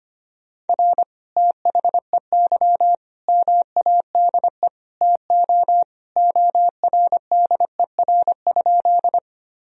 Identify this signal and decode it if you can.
Morse code, heard as a keyed tone, about 25 wpm, 700 Hz. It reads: R THEY MADE TO ORDER?
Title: Sentences from Top 400 Words and Abbreviations 25wpm